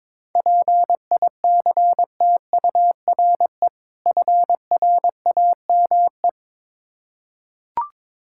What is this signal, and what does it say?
Morse code, heard as a keyed tone, about 22 wpm, 700 Hz. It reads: PICTURE FRAME
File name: Difficult Words with Hints - 22wpm